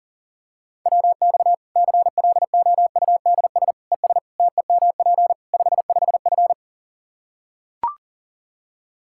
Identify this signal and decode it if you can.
Morse code, heard as a keyed tone, about 40 wpm, 700 Hz. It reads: WX CLOUDS ES TEMP 55F
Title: K1USN SST QSO Practice 40wpm